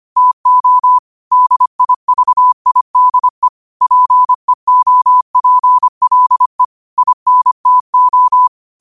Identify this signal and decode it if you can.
Morse code, heard as a keyed tone, about 25 wpm, 1000 Hz. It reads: TO DIVIDE PEOPLE INTO
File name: qotdfiles / 25wpm